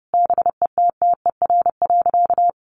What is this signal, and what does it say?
Morse code, heard as a keyed tone, about 30 wpm, 700 Hz. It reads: BETTER.
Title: Sentences from Top 400 Words 30wpm